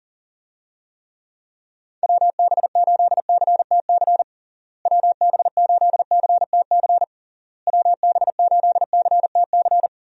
Morse code, heard as a keyed tone, about 37 wpm, 700 Hz. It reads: WB8CTC WB8CTC WB8CTC
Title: USA Call Signs - 6 Characters - Repeated 3x - 40wpm